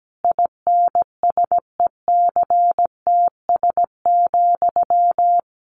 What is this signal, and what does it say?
Morse code, heard as a keyed tone, about 17 wpm, 700 Hz. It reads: INSECTS,
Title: A Sound Of Thunder 17wpm